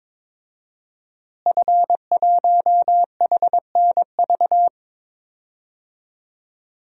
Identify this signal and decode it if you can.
Morse code, heard as a keyed tone, about 22 wpm, 700 Hz. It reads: F1HNV
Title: DXCC-100 Call Signs - 5 Characters - Repeated 2x - 22wpm